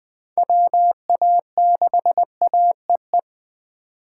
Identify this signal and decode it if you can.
Morse code, heard as a keyed tone, about 21 wpm, 700 Hz. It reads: WA6AEE
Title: K1USN SST QSO Practice - 20wpm